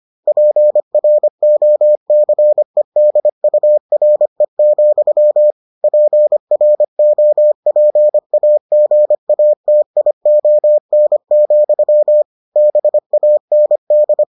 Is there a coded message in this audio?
Yes: Morse code from 0.3 to 14.2 s, heard as a keyed tone, about 25 wpm, 600 Hz. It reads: PROCEDURE, PROPAGATION, BAND